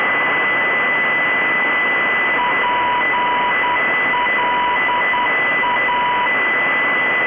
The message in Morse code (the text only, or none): PLA